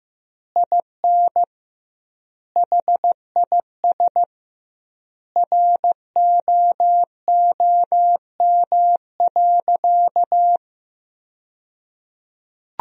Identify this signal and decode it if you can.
Morse code, heard as a keyed tone, about 15 wpm, 700 Hz. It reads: IN HIS ROOM.
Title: 2x Word-Spacing - Sentences from Top 300 Words - 15wpm